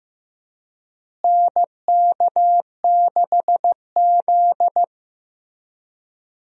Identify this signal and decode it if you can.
Morse code, heard as a keyed tone, about 15 wpm, 700 Hz. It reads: NK6Z
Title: USA Call Signs - 4 Characters - Repeated 3x - 15wpm